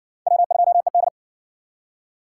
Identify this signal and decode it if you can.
Morse code, heard as a keyed tone, about 50 wpm, 700 Hz. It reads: W2ED